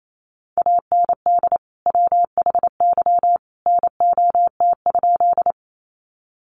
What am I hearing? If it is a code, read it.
Morse code, heard as a keyed tone, about 28 wpm, 700 Hz. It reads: AND WHY NOT?
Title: Sentences from Top 600 Words 28wpm